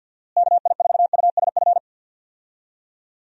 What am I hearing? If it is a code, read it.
Morse code, heard as a keyed tone, about 52 wpm, 700 Hz. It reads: KI4USF
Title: USA Call Signs - 6 Characters - Repeated 2x - 50wpm